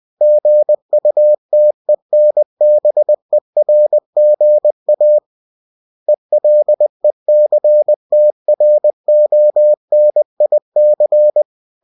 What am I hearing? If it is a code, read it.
Morse code, heard as a keyed tone, about 20 wpm, 600 Hz. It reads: GUTENBERGA ELECTRONIC
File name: cw-books / winnie-the-pooh / winnie-20wpm-page-0112